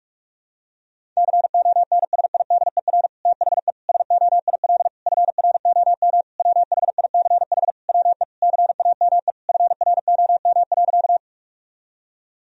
Morse code, heard as a keyed tone, about 46 wpm, 700 Hz. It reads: CONSIDER THE SOIL FROM WHICH WE CAME FROM.
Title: Sentences from Top 1000 Words - Review Part 4 - 45wpm